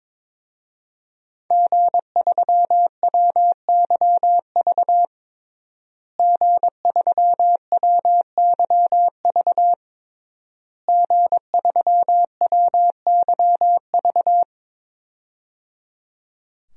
Morse code, heard as a keyed tone, about 22 wpm, 700 Hz. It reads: G3WYV G3WYV G3WYV
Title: DXCC-100 Call Signs - 5 Characters - Repeated 3x - 22wpm